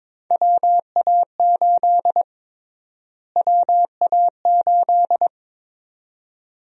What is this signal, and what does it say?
Morse code, heard as a keyed tone, about 22 wpm, 700 Hz. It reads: WA8 WA8
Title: USA Call Sign Prefixes - Repeated 2x - 22wpm